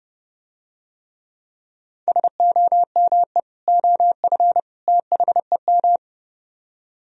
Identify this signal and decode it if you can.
Morse code, heard as a keyed tone, about 30 wpm, 700 Hz. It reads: SOME OF THEM